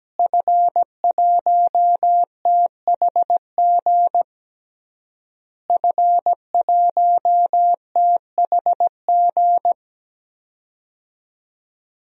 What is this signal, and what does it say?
Morse code, heard as a keyed tone, about 17 wpm, 700 Hz. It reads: F1THG F1THG
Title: DXCC-100 Call Signs - 5 Characters - Repeated 3x - 17wpm